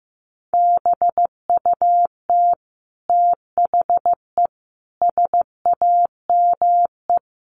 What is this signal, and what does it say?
Morse code, heard as a keyed tone, about 15 wpm, 700 Hz. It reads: BUT THE SAME
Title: A Sound Of Thunder 15wpm